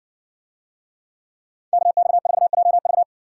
Morse code, heard as a keyed tone, about 60 wpm, 700 Hz. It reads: KX4JV